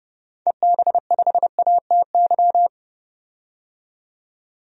Morse code, heard as a keyed tone, about 30 wpm, 700 Hz. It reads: EB5ATY
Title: DXCC-100 Call Signs - 6 Characters - 30wpm